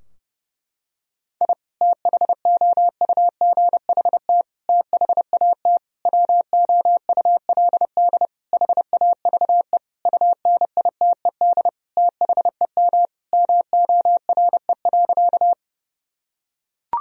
Morse code, heard as a keyed tone, about 30 wpm, 700 Hz. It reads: I THOUGHT THAT WOULD HAVE UNITED THEM MORE.
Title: Sentences from Top 1000 Words - Review Part 4 - 30wpm